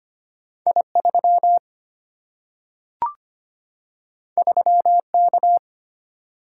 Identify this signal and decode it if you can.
Morse code, heard as a keyed tone, about 25 wpm, 700 Hz. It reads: I3 3K
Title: Single-Letter Single-Number 25wpm